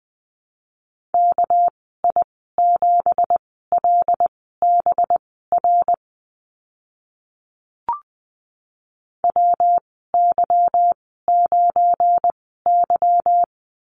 Morse code, heard as a keyed tone, about 19 wpm, spaced as for 14 wpm, 700 Hz. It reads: KI7LBR WY9Y
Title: 238 Lesson 38g - Letter q - Call Signs 20wpm